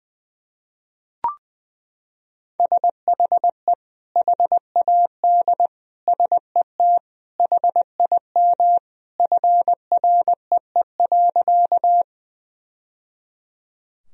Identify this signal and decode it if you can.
Morse code, heard as a keyed tone, about 20 wpm, 700 Hz. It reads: SHE HAD SET HIM FREE.
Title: Sentences from Top 1000 Words - Review Part 1 - 20wpm